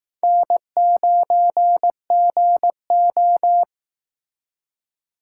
Morse code, heard as a keyed tone, about 18 wpm, 700 Hz. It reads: N9GO
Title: Speed Racing - USA Call Signs 15wpm